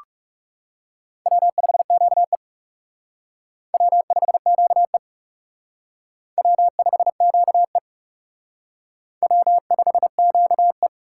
Morse code, heard as a keyed tone, about 36 wpm, 700 Hz. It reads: W5QE W5QE W5QE W5QE
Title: Speed Racing - USA 1x2 Call Signs 30wpm